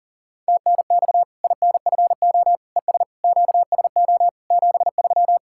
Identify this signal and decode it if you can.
Morse code, heard as a keyed tone, about 37 wpm, 700 Hz. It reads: TNX INFO ES QSO 73
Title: K1USN SST QSO Practice 40wpm